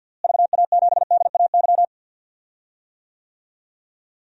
Morse code, heard as a keyed tone, about 49 wpm, 700 Hz. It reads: VA7DAY
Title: DXCC-100 Call Signs - 6 Characters - Repeated 3x - 50wpm